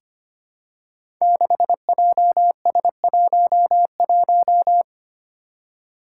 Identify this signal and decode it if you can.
Morse code, heard as a keyed tone, about 25 wpm, 700 Hz. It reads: 6JS11
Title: Five Challenging Letter-Number Combinations 25wpm